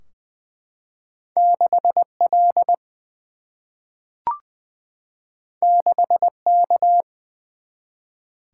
Morse code, heard as a keyed tone, about 20 wpm, 700 Hz. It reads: 6 L 6 K
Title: Single-Letter Single-Number 20wpm